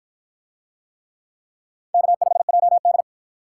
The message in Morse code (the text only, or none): K5JD